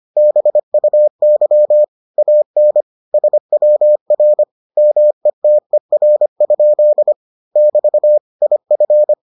Morse code, heard as a keyed tone, about 25 wpm, 600 Hz. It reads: BUY AN SWR METER? <BT> IF